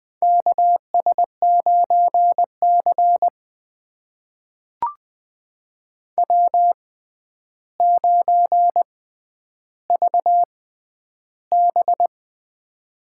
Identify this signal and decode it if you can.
Morse code, heard as a keyed tone, about 20 wpm, spaced as for 10 wpm, 700 Hz. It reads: KS9C W9VB
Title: USA Call Signs - 4 Characters - 6x Character Spacing - 20wpm